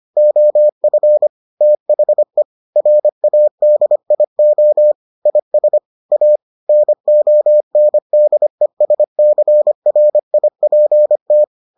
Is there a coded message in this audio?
Yes: Morse code from 0.2 to 11.4 s, heard as a keyed tone, about 25 wpm, 600 Hz. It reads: OF THE RADIO IS A NONDESCRIPT